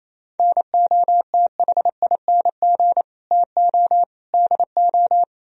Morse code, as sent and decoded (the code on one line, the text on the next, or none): -. --- - .... .. -. --. / - --- / -.. ---
NOTHING TO DO